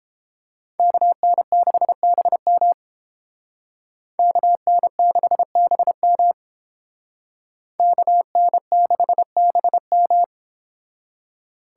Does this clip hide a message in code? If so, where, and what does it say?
Yes: Morse code from 0.8 to 10.2 s, heard as a keyed tone, about 30 wpm, 700 Hz. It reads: KN6BM KN6BM KN6BM